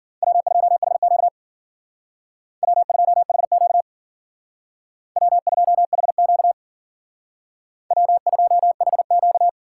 Morse code, heard as a keyed tone, about 52 wpm, 700 Hz. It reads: W2HQ W2HQ W2HQ W2HQ